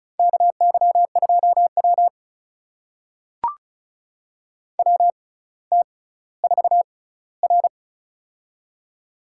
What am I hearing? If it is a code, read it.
Morse code, heard as a keyed tone, about 36 wpm, spaced as for 18 wpm, 700 Hz. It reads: KY2W WT4R
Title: USA Call Signs - 4 Characters - 6x Character Spacing - 35wpm